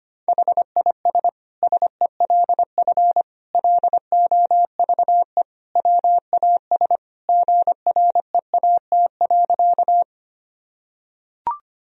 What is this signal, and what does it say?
Morse code, heard as a keyed tone, about 25 wpm, 700 Hz. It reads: HIS SELF LOVE WAS GREAT.
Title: Sentences from Top 1000 Words - Review Part 1 - 25wpm